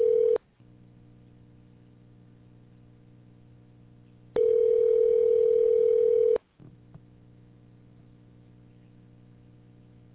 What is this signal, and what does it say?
Ringback tone (North America), 440 + 480 Hz.